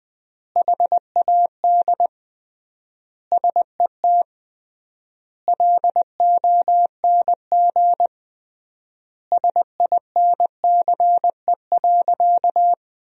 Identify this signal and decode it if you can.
Morse code, heard as a keyed tone, about 21 wpm, 700 Hz. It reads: HAD SET LONG SINCE.
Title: 3x Word-Spacing - Sentences from Top 300 Words - 20wpm